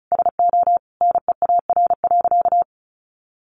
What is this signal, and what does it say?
Morse code, heard as a keyed tone, about 34 wpm, 700 Hz. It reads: SO NEAR.
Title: Sentences from Top 300 Words 35wpm